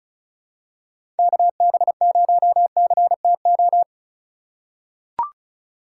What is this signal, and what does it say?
Morse code, heard as a keyed tone, about 36 wpm, 700 Hz. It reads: KB0CTO
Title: USA Call Signs - 6 Characters - 2x Character Spacing - 35wpm